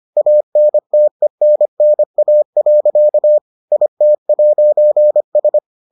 Morse code, heard as a keyed tone, about 25 wpm, 600 Hz. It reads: ANTENNA. IT'S